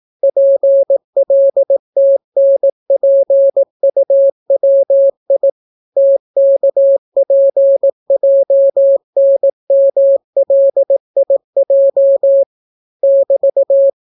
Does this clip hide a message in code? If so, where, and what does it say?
Yes: Morse code from 0.2 to 13.9 s, heard as a keyed tone, about 18 wpm, 550 Hz. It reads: PLTNPUWI TKPJNMLIJ <BT>